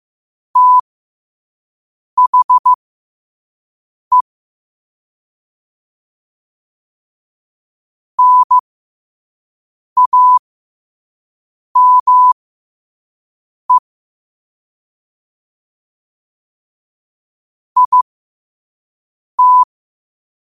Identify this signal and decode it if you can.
Morse code, heard as a keyed tone, about 15 wpm, spaced as for 5 wpm, 1000 Hz. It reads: THE NAME IT
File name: qotdfiles / 5wpm